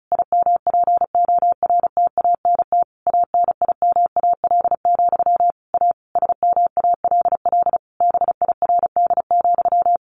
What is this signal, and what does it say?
Morse code, heard as a keyed tone, about 34 wpm, 700 Hz. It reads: IMPORTANT ANIMAL, A SMALL BIRD,
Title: A Sound Of Thunder 35wpm